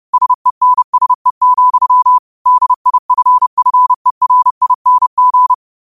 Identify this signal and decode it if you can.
Morse code, heard as a keyed tone, about 30 wpm, 1000 Hz. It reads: SENSE, DIFFERING
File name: qotdfiles / 30wpm